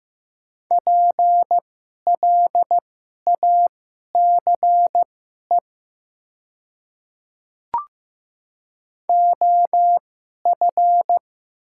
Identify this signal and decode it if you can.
Morse code, heard as a keyed tone, about 15 wpm, spaced as for 11 wpm, 700 Hz. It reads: PLACE OF
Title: 2x Character Spacing - Top 500 Words - 15wpm